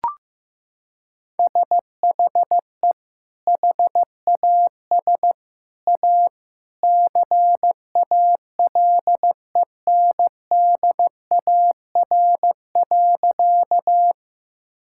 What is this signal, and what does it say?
Morse code, heard as a keyed tone, about 15 wpm, 700 Hz. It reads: SHE HAS A CALENDAR.
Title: Sentences from Top 1600 Words 15wpm